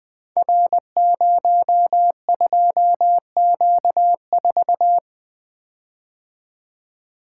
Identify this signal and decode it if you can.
Morse code, heard as a keyed tone, about 20 wpm, 700 Hz. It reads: R02Q4
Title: Code Groups - 5 Letter:Numbers - 20wpm